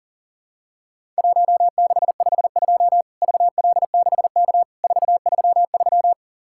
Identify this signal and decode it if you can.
Morse code, heard as a keyed tone, about 40 wpm, 700 Hz. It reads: 1652 VL6K 433
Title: Worked All Europe DX Contest - QTC Exchange - 40wpm